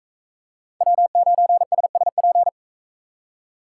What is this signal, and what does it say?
Morse code, heard as a keyed tone, about 44 wpm, 700 Hz. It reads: W9SSP